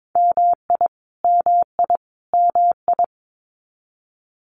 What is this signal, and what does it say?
Morse code, heard as a keyed tone, about 22 wpm, 700 Hz. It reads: MI MI MI
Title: US States Repeated 3x 22wpm